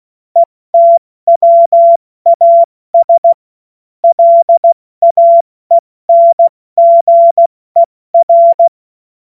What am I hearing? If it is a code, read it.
Morse code, heard as a keyed tone, about 16 wpm, 680 Hz. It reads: ETWAS LAENGER
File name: Morsekurs / Witze / Witz 07. Dezember 2021, 70 F80